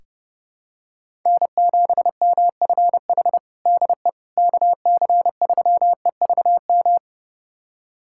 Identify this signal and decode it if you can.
Morse code, heard as a keyed tone, about 32 wpm, 700 Hz. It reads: N7MFH DE KC3EVM
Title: USA Callsign DE Callsign 30wpm